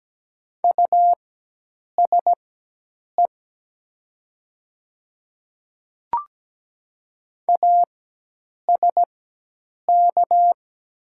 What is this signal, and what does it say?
Morse code, heard as a keyed tone, about 17 wpm, spaced as for 8 wpm, 700 Hz. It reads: USE ASK
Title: 4x Character Spacing - Top 200 Words - 17wpm